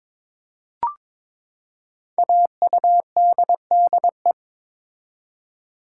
Morse code, heard as a keyed tone, about 22 wpm, 700 Hz. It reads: AUDDE
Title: Code Groups - 5 Letters - 22wpm